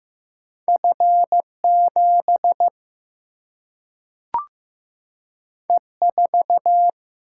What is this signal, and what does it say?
Morse code, heard as a keyed tone, about 15 wpm, 700 Hz. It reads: F 7 E 4
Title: Single-Letter Single-Number 15wpm